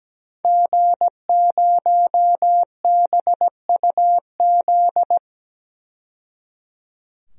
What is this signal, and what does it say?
Morse code, heard as a keyed tone, about 17 wpm, 700 Hz. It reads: G0BUZ
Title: DXCC-100 Call Signs - 5 Characters - 17wpm